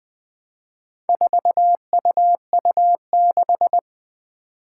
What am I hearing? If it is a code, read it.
Morse code, heard as a keyed tone, about 20 wpm, 700 Hz. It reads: 4UU6
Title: Four Challenging Letter-Number Combinations 20wpm